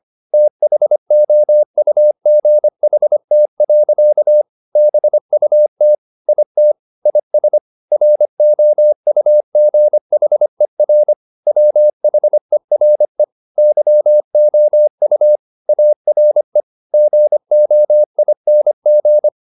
Morse code, heard as a keyed tone, about 25 wpm, 600 Hz. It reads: THOUGHT. BUT IT IS ROUGHER WHERE YOU ARE GOING
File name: the old man and the sea part 1 25wpm